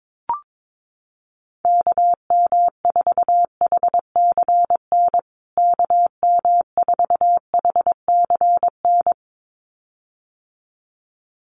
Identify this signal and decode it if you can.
Morse code, heard as a keyed tone, about 22 wpm, 700 Hz. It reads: KM4HCN KM4HCN
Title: USA Call Signs Repeated 2x 22wpm